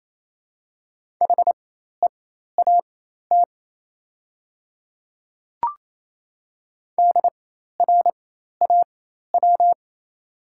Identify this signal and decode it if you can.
Morse code, heard as a keyed tone, about 28 wpm, spaced as for 13 wpm, 690 Hz. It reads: HEAT DRAW
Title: 4x Character Spacing - Top 500 Words - 28wpm